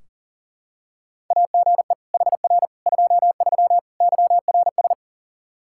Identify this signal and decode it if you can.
Morse code, heard as a keyed tone, about 40 wpm, 700 Hz. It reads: AGE HR 23 YRS
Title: K1USN SST QSO Practice 40wpm